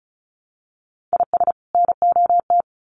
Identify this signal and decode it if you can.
Morse code, heard as a keyed tone, about 34 wpm, 700 Hz. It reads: IS NOT
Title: Sets of 2 Words 35wpm